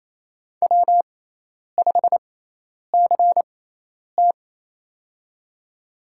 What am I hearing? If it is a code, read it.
Morse code, heard as a keyed tone, about 28 wpm, spaced as for 10 wpm, 700 Hz. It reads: W5CT
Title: USA Call Signs - 4 Characters - 6x Character Spacing - 28wpm